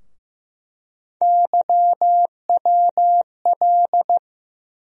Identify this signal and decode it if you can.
Morse code, heard as a keyed tone, about 15 wpm, 700 Hz. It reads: YWL
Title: USA Call Sign Postfixes - 3 Letters 15wpm